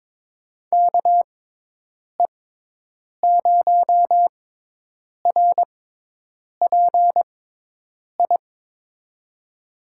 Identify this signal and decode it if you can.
Morse code, heard as a keyed tone, about 23 wpm, spaced as for 8 wpm, 700 Hz. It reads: KE0RPI
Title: USA Call Signs - 6 Characters - 6x Character Spacing - 22wpm